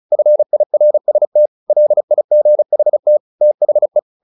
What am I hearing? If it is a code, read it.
Morse code, heard as a keyed tone, about 34 wpm, 600 Hz. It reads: FIRST LIGHT THE
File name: the old man and the sea / the old man and the sea 1 35wpm